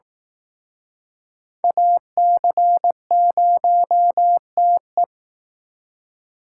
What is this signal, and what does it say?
Morse code, heard as a keyed tone, about 18 wpm, 700 Hz. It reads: AC0TE
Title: Speed Racing - USA Call Signs 15wpm